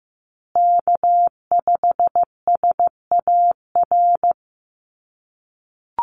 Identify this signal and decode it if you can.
Morse code, heard as a keyed tone, about 15 wpm, 700 Hz. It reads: K5SAR
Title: USA Call Signs Repeated 3x 15wpm